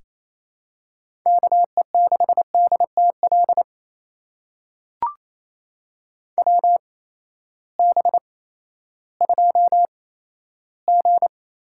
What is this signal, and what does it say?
Morse code, heard as a keyed tone, about 28 wpm, spaced as for 14 wpm, 700 Hz. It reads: KE6DTL WB2G
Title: USA Call Signs - 6 Characters - 8x Character Spacing - 28wpm